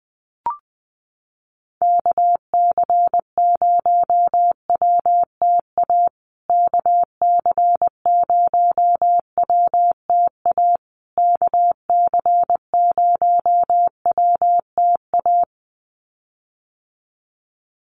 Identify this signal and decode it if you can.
Morse code, heard as a keyed tone, about 20 wpm, 700 Hz. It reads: KC0WTA KC0WTA KC0WTA
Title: USA Call Signs Repeated 3x 20wpm